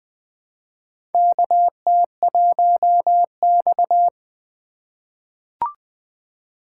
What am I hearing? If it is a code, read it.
Morse code, heard as a keyed tone, about 20 wpm, 700 Hz. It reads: KT1X
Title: USA Call Signs - 4 Characters - 4x Character Spacing - 20wpm